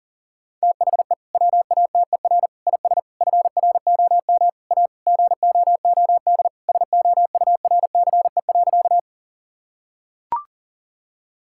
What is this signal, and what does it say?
Morse code, heard as a keyed tone, about 40 wpm, 700 Hz. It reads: THE WATER IS FROM A GOOD SOURCE.
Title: Sentences from Top 1000 Words - Review Part 5 - 40wpm